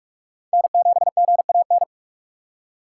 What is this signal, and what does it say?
Morse code, heard as a keyed tone, about 46 wpm, 700 Hz. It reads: N7GAN
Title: USA Call Signs - 5 Characters - 4x Character Spacing - 45wpm